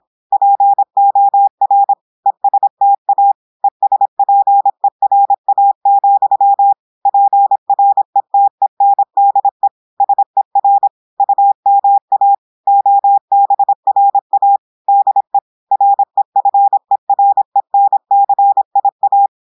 Morse code, heard as a keyed tone, about 26 wpm, 800 Hz. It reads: POR ESTA ESPERA, PRETENDE SER UMA OBRA DE REFERENCIA
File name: audios / portugais / portugais texte en clair 1 vitesse 26 mots minutes